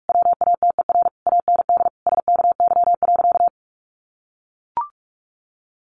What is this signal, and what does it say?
Morse code, heard as a keyed tone, about 44 wpm, 700 Hz. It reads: WATER AND SKY.
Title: Sentences from Top 1200 Words 45wpm